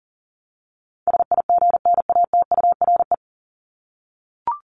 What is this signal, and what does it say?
Morse code, heard as a keyed tone, about 40 wpm, 700 Hz. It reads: SIGNATURE
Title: Top 1751-2000 Words 40wpm